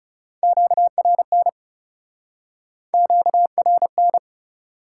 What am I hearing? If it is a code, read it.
Morse code, heard as a keyed tone, about 40 wpm, 700 Hz. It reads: QRN QRN